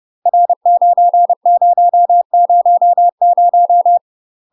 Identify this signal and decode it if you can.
Morse code, heard as a keyed tone, about 30 wpm, 700 Hz. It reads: R9000
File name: UK MASTER QSOs TRAINING / 30WPM / UK MASTER QSOs 0045